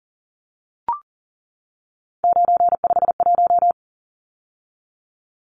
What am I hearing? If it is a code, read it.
Morse code, heard as a keyed tone, about 40 wpm, 700 Hz. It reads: 951
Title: Numbers - 3-Digit 40wpm